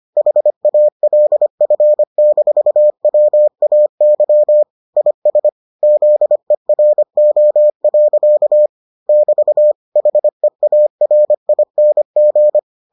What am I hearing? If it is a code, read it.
Morse code, heard as a keyed tone, about 25 wpm, 600 Hz. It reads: HALF-WAY IS ZERO. <BT> HEARING